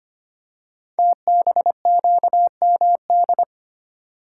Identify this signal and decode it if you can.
Morse code, heard as a keyed tone, about 25 wpm, 700 Hz. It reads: TBQMD
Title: Code Groups - 5 Letters - 25wpm